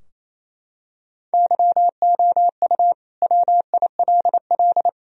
Morse code, heard as a keyed tone, about 28 wpm, 700 Hz. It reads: YOU WILL
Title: Sentences from Top 1000 Words - Review Part 4 - 28wpm